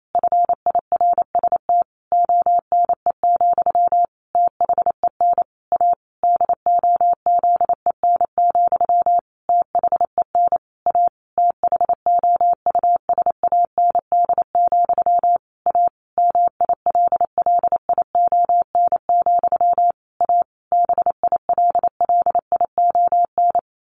Morse code, heard as a keyed tone, about 28 wpm, 700 Hz. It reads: FIRST ONE, THEN A DOZEN, THEN A THOUSAND, A MILLION, A BILLION
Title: A Sound Of Thunder 28wpm